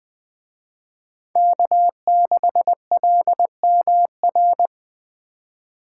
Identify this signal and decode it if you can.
Morse code, heard as a keyed tone, about 23 wpm, 700 Hz. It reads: K6LMR
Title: Speed Racing - USA Call Signs 20wpm